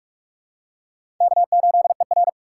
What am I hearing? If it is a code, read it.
Morse code, heard as a keyed tone, about 44 wpm, 700 Hz. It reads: K8ER
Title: USA Call Signs - 4 Characters - 6x Character Spacing - 45wpm